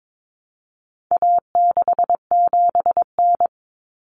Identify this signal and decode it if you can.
Morse code, heard as a keyed tone, about 22 wpm, 700 Hz. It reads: A67N
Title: Common Cut Numbers - 4-Digit 22wpm